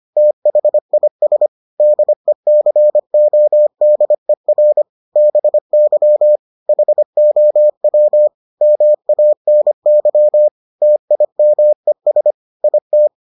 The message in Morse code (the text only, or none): THIS DECODER BY HOW MANY TIMES IT